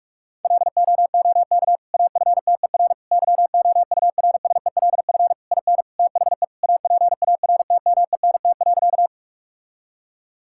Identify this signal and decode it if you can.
Morse code, heard as a keyed tone, about 46 wpm, 700 Hz. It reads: LOOK AFTER YOURSELF IN THE APARTMENT.
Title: Sentences from Top 1000 Words - Review Part 6 - 45wpm